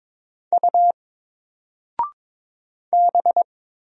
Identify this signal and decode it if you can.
Morse code, heard as a keyed tone, about 22 wpm, 700 Hz. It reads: U B
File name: Single-Letter-Number-22wpm